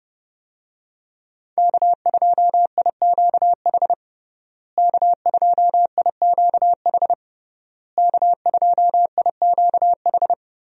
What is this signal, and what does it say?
Morse code, heard as a keyed tone, about 30 wpm, 700 Hz. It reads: K2IQH K2IQH K2IQH
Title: USA Call Signs - 5 Characters - Repeated 3x - 30wpm